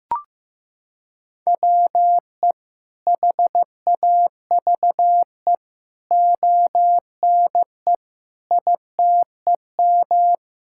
Morse code, heard as a keyed tone, about 15 wpm, 700 Hz. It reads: WE HAVE ONE ITEM
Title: Sentences from Top 1000 Words - Review Part 6 - 15wpm